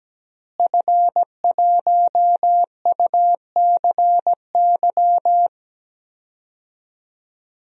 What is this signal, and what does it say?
Morse code, heard as a keyed tone, about 17 wpm, 700 Hz. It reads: F1UCY
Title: DXCC-100 Call Signs - 5 Characters - Repeated 3x - 17wpm